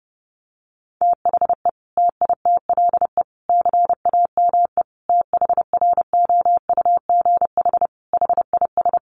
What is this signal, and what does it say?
Morse code, heard as a keyed tone, about 30 wpm, 700 Hz. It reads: THE TITLE CAME THROUGH HIS
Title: Sentences from Top 1200 Words 30wpm